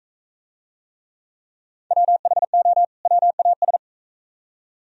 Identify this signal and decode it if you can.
Morse code, heard as a keyed tone, about 40 wpm, 700 Hz. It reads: WHO WAS